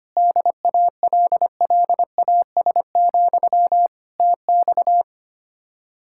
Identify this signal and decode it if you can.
Morse code, heard as a keyed tone, about 24 wpm, 700 Hz. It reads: DALLAS, TX
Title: Speed-Racing Common QSO Phrases - 25wpm